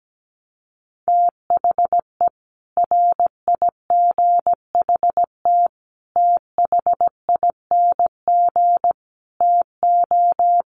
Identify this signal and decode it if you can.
Morse code, heard as a keyed tone, about 17 wpm, 700 Hz. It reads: THE RIGHT THING TO
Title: Sets of 5 Words 17wpm